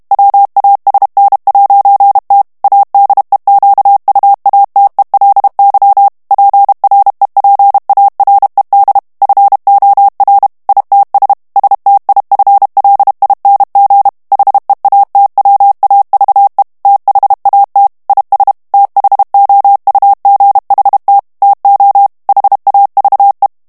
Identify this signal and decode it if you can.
Morse code, heard as a keyed tone, about 32 wpm, 800 Hz. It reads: WASN'T ADEQUATELY PREPARED FOR ITS STIFLING HEATWAVE THAT IS THOUGHT TO HAVE
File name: ANGLAIS texte clair 2 vitesse 30 mots minute